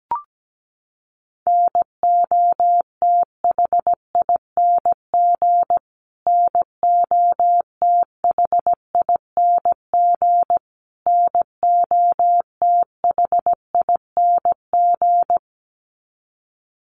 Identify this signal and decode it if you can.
Morse code, heard as a keyed tone, about 17 wpm, 700 Hz. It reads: NOTHING NOTHING NOTHING
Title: Top 500 Words Repeated 17wpm